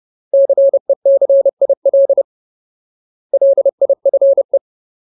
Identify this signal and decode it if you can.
Morse code, heard as a keyed tone, about 30 wpm, 550 Hz. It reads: CECIL LIFE
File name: Ultimate Word List / 30WPM Ultimate